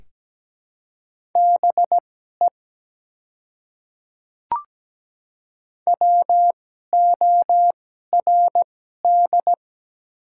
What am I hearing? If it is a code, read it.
Morse code, heard as a keyed tone, about 17 wpm, spaced as for 12 wpm, 700 Hz. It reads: BE WORD
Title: 2x Character Spacing - Top 100 Words - 17wpm